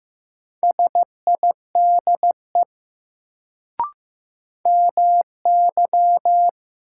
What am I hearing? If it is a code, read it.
Morse code, heard as a keyed tone, about 15 wpm, 700 Hz. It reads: SIDE MY
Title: Sets of 2 Words - New Word Drill - 15wpm